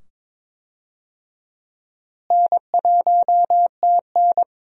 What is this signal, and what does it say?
Morse code, heard as a keyed tone, about 22 wpm, 700 Hz. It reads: N1TN
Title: USA Call Signs - 4 Characters - Repeated 2x - 22wpm